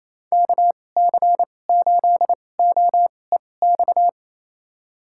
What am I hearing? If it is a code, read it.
Morse code, heard as a keyed tone, about 28 wpm, spaced as for 20 wpm, 700 Hz. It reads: KC8OEX
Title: USA Call Signs - 6 Characters - 2x Character Spacing - 28wpm